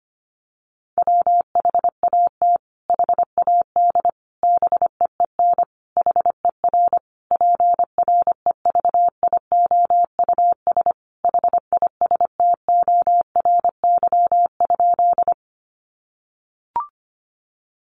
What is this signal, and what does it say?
Morse code, heard as a keyed tone, about 25 wpm, 700 Hz. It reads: WHAT HAD BEEN HER PREVIOUS HISTORY?
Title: Sentences from Top 1500 Words 25wpm